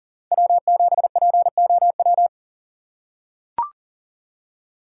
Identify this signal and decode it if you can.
Morse code, heard as a keyed tone, about 40 wpm, 700 Hz. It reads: W7POW
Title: USA Call Signs - 5 Characters - 6x Character Spacing - 40wpm